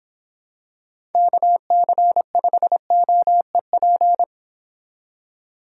Morse code, heard as a keyed tone, about 26 wpm, 700 Hz. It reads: KC5OEP